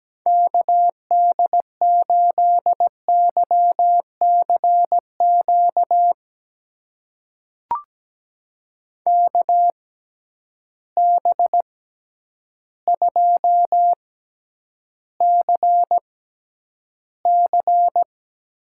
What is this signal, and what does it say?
Morse code, heard as a keyed tone, about 17 wpm, spaced as for 9 wpm, 700 Hz. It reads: KD8YCQ KB2CC
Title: USA Call Signs - 6 Characters - 6x Character Spacing - 17wpm